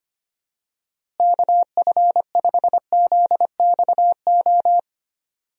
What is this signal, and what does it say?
Morse code, heard as a keyed tone, about 25 wpm, 700 Hz. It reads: KF5ZXO
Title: USA Call Signs - 6 Characters - Repeated 3x - 25wpm